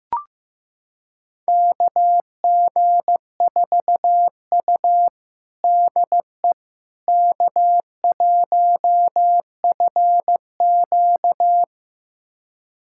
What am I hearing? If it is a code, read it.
Morse code, heard as a keyed tone, about 15 wpm, 700 Hz. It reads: KG4U DE K1FQ
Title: USA Callsign DE Callsign - 4 Characters - 15wpm